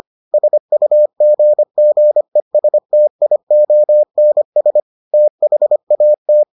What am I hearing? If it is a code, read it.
Morse code, heard as a keyed tone, about 25 wpm, 600 Hz. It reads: SUGGESTIONS THAT